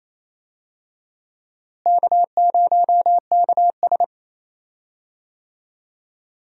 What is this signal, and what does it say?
Morse code, heard as a keyed tone, about 27 wpm, 700 Hz. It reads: K0KS
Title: USA Call Signs - 4 Characters - Repeated 2x - 28wpm